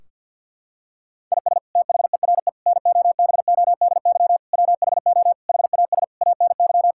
Morse code, heard as a keyed tone, about 50 wpm, 700 Hz. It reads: IS THERE NOBODY WHO HAS ANY
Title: Sentences from Top 1600 Words 50wpm